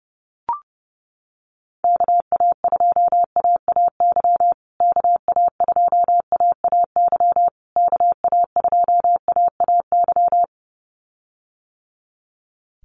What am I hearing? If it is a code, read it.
Morse code, heard as a keyed tone, about 30 wpm, 700 Hz. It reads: KA2AAY KA2AAY KA2AAY
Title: USA Call Signs Repeated 3x 30wpm